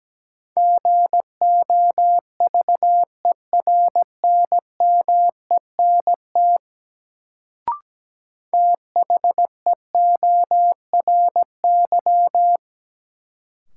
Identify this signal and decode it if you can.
Morse code, heard as a keyed tone, about 17 wpm, 700 Hz. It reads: GOVERNMENT THEORY
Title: Sentences from Top 600 Words - New Word Drill - 17wpm